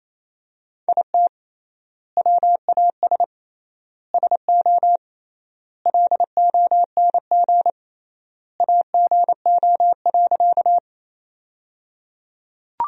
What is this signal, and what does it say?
Morse code, heard as a keyed tone, about 28 wpm, 700 Hz. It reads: IT WAS SO LONG AGO.
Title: 3x Word-Spacing - Sentences from Top 500 Words - 28wpm